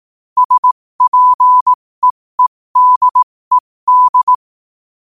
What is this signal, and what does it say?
Morse code, heard as a keyed tone, about 18 wpm, spaced as for 15 wpm, 1000 Hz. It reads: SPEEDED
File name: qotdfiles / 15wpm